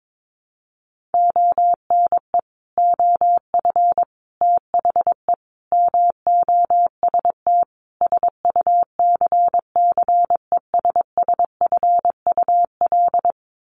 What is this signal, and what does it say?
Morse code, heard as a keyed tone, about 22 wpm, 700 Hz. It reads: ONE OF THE MOST SUCCESSFUL
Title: Sets of 5 Words Encore2 22wpm